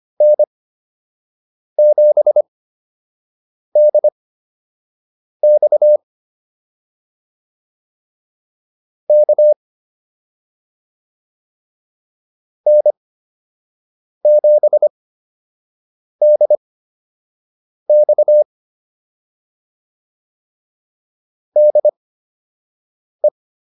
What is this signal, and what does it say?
Morse code, heard as a keyed tone, about 25 wpm, spaced as for 6 wpm, 600 Hz. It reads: N7DX K N7DX DE
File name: QSO 102 06